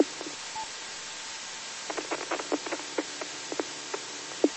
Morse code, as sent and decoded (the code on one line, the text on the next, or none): none